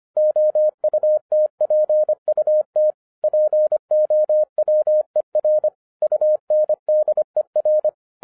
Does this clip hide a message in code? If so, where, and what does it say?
Yes: Morse code from 0.2 to 7.9 s, heard as a keyed tone, about 25 wpm, 610 Hz. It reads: OUTPUT POWER UNDER